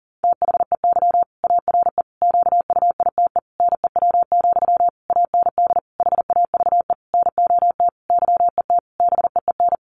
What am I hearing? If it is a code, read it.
Morse code, heard as a keyed tone, about 40 wpm, 700 Hz. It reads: THEY ARE QUITE NEW, AND HAVE NOT YET BEEN
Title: Sentences from Top 700 Words 40wpm